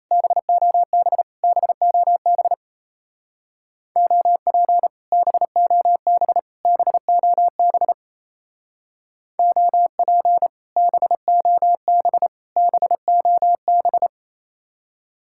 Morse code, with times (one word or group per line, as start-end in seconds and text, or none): 0.1-1.2 BOB
1.4-2.5 BOB
4.0-4.9 OP
5.1-6.4 BOB
6.7-7.9 BOB
9.4-10.5 OP
10.8-12.3 BOB
12.6-14.1 BOB